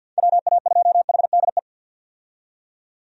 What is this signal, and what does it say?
Morse code, heard as a keyed tone, about 53 wpm, 700 Hz. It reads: WA2HDE
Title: USA Call Signs - 6 Characters - 50wpm